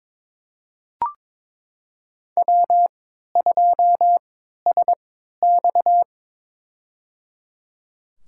Morse code, heard as a keyed tone, about 22 wpm, spaced as for 13 wpm, 700 Hz. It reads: W2SX
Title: USA Call Signs - 4 Characters - 3x Character Spacing - 22wpm